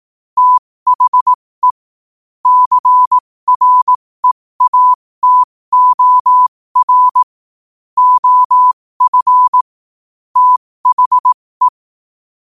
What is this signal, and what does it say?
Morse code, heard as a keyed tone, about 18 wpm, spaced as for 15 wpm, 1000 Hz. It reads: THE CREATOR OF THE
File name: qotdfiles / 15wpm